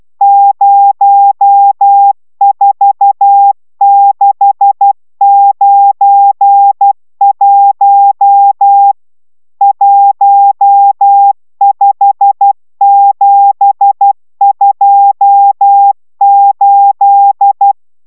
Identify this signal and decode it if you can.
Morse code, heard as a keyed tone, about 12 wpm, 800 Hz. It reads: 04691 15728